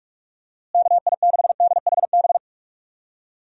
Morse code, heard as a keyed tone, about 44 wpm, 700 Hz. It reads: KI6DHB
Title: USA Call Signs - 6 Characters - Repeated 3x - 45wpm